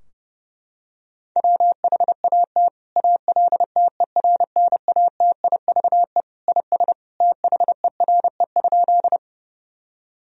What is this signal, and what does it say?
Morse code, heard as a keyed tone, about 30 wpm, 700 Hz. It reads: WHAT ALTERNATIVE IS THERE?
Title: Sentences from Top 1000 Words - Review Part 4 - 30wpm